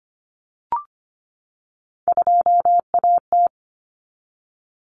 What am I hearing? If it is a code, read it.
Morse code, heard as a keyed tone, about 25 wpm, 700 Hz. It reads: 2AT
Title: Common Cut Numbers - 3-Digit 25wpm